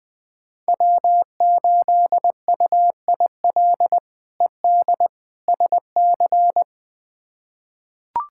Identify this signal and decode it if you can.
Morse code, heard as a keyed tone, about 20 wpm, 700 Hz. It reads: W8UIL ED SC
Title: K1USN SST QSO Practice - 20wpm